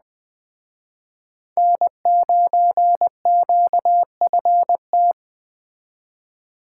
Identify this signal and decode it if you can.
Morse code, heard as a keyed tone, about 21 wpm, 700 Hz. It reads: N9QFT
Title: Speed Racing - USA 1x3 Call Signs 17wpm